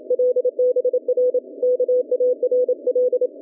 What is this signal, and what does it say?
Morse code, heard as a keyed tone, about 30 wpm, 440 Hz. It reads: LBR KARL